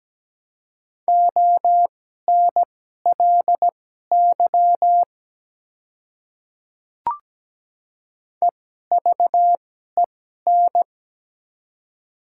Morse code, heard as a keyed tone, about 17 wpm, spaced as for 12 wpm, 700 Hz. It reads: ONLY EVEN
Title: 2x Character Spacing - Top 200 Words - 17wpm